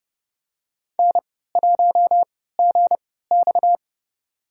Morse code, heard as a keyed tone, about 31 wpm, spaced as for 17 wpm, 700 Hz. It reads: N1GX
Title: USA Call Signs - 4 Characters - 3x Character Spacing - 30wpm